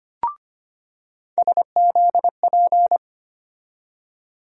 Morse code, heard as a keyed tone, about 25 wpm, 700 Hz. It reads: SZP
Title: USA Call Sign Postfixes - 3 Letters 25wpm